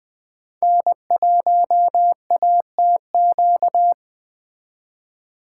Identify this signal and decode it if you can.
Morse code, heard as a keyed tone, about 18 wpm, 700 Hz. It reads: N1ATQ
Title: Speed Racing - USA Call Signs 15wpm